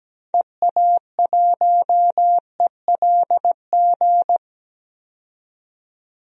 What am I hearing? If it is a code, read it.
Morse code, heard as a keyed tone, about 17 wpm, 700 Hz. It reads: EA1ELG
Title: DXCC-100 Call Signs - 6 Characters - 17wpm